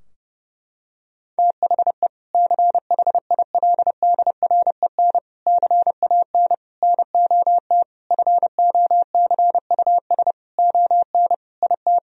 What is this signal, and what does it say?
Morse code, heard as a keyed tone, about 30 wpm, 700 Hz. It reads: THE CHILDREN CAN NOT FOCUS ON IT
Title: Sentences from Top 1000 Words - Review Part 4 - 30wpm